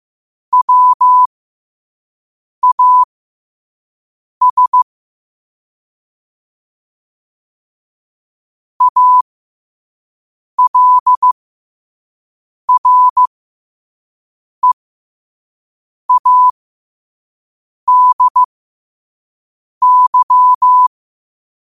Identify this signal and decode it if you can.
Morse code, heard as a keyed tone, about 15 wpm, spaced as for 5 wpm, 1000 Hz. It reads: WAS ALREADY
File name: qotdfiles / 5wpm